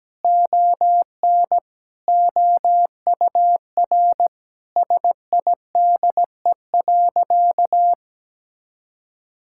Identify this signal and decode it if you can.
Morse code, heard as a keyed tone, about 17 wpm, 700 Hz. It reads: ON OUR SIDE.
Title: Sentences from Top 1000 Words - Review Part 6 - 17wpm